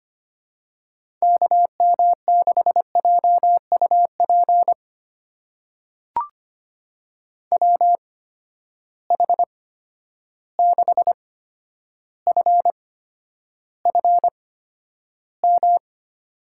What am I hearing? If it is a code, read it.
Morse code, heard as a keyed tone, about 25 wpm, spaced as for 11 wpm, 700 Hz. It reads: KM6JUP WH6FFM
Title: USA Call Signs - 6 Characters - 8x Character Spacing - 25wpm